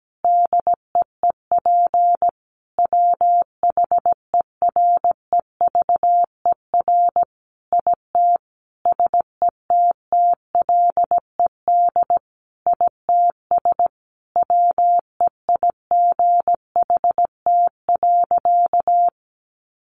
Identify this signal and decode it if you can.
Morse code, heard as a keyed tone, about 17 wpm, 700 Hz. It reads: DEEP WHEREVER IT SETTLED ITS WEIGHT.
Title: A Sound Of Thunder 17wpm